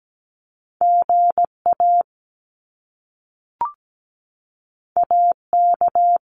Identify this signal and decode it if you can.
Morse code, heard as a keyed tone, about 17 wpm, 700 Hz. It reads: GA AK
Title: US States 17wpm